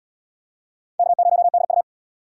Morse code, heard as a keyed tone, about 78 wpm, 700 Hz. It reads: K0DU